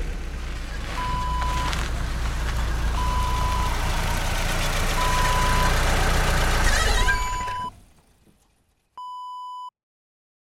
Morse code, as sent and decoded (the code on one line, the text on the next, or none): none